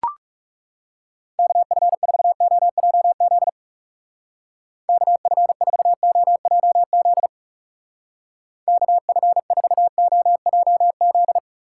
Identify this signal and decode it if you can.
Morse code, heard as a keyed tone, about 40 wpm, 700 Hz. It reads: KF4OJZ KF4OJZ KF4OJZ